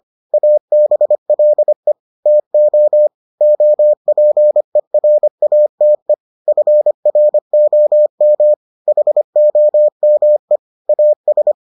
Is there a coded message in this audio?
Yes: Morse code from 0.3 to 11.5 s, heard as a keyed tone, about 25 wpm, 600 Hz. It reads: ABLE TO OPERATE FROM HOME AS